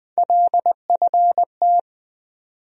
Morse code, heard as a keyed tone, about 20 wpm, 700 Hz. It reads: LFT
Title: USA Call Sign Postfixes - 3 Letters 20wpm